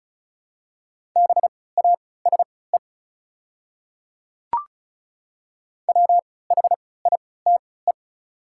Morse code, heard as a keyed tone, about 34 wpm, spaced as for 20 wpm, 700 Hz. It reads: BASE WHITE
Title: 3x Character Spacing - Top 500 Words - 35wpm